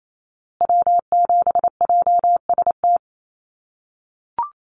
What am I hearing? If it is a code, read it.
Morse code, heard as a keyed tone, about 28 wpm, 700 Hz. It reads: W7JST
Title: USA Call Signs Repeated 2x 28wpm